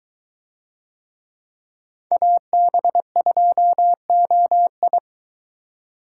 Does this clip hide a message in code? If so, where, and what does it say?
Yes: Morse code from 2.1 to 5.0 s, heard as a keyed tone, about 23 wpm, 700 Hz. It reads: AB2OI